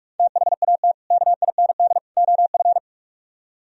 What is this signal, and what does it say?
Morse code, heard as a keyed tone, about 48 wpm, 700 Hz. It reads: THAT KIND OF